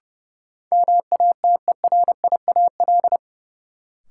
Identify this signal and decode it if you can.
Morse code, heard as a keyed tone, about 32 wpm, 700 Hz. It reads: MATERIAL
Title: Sentences from Top 1000 Words - Review Part 5 - New Word Drill - 30wpm